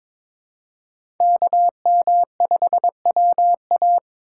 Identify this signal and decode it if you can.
Morse code, heard as a keyed tone, about 22 wpm, 700 Hz. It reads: KM5WA
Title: USA Call Signs - 2x Character Spacing - 22wpm